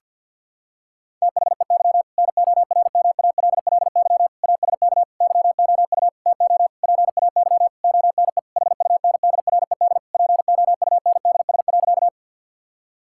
Morse code, heard as a keyed tone, about 52 wpm, 700 Hz. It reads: THEY NORMALLY ASK YOU TO PAY ONE HUNDRED POUNDS.
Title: Sentences from Top 1000 Words - Review Part 6 - 50wpm